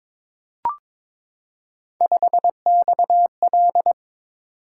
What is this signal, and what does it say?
Morse code, heard as a keyed tone, about 22 wpm, 700 Hz. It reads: 5XL
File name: practice / 22wpm / Three Letter-Number Combinations - 22wpm